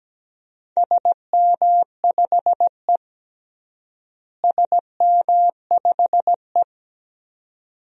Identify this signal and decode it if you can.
Morse code, heard as a keyed tone, about 17 wpm, 700 Hz. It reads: SM5E SM5E
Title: DXCC-100 Call Signs - 4 Characters - Repeated 2x - 17wpm